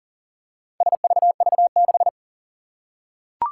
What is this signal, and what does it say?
Morse code, heard as a keyed tone, about 40 wpm, 700 Hz. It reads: SVV6
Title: Four Challenging Letter-Number Combinations 40wpm